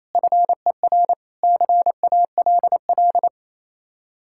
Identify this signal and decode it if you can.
Morse code, heard as a keyed tone, about 37 wpm, 700 Hz. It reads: FER CALL